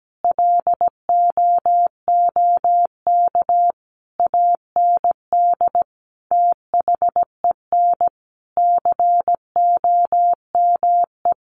Morse code, heard as a keyed tone, about 17 wpm, 700 Hz. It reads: LOOK AND THEN COME
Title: Sentences from Top 600 Words 17wpm